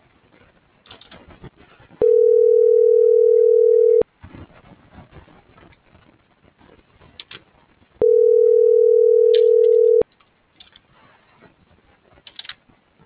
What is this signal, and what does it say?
Ringback tone (North America), 440 + 480 Hz.